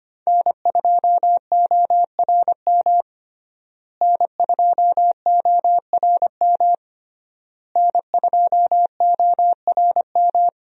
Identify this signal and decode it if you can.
Morse code, heard as a keyed tone, about 25 wpm, 700 Hz. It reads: N2ORM N2ORM N2ORM
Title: USA Call Signs - 5 Characters - Repeated 3x - 25wpm